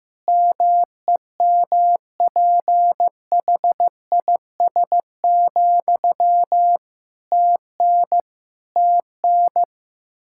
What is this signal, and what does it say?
Morse code, heard as a keyed tone, about 16 wpm, 700 Hz. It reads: MEMPHIS, TN TN
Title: Practice QSO - Exchange 1 - Beginning - 15wpm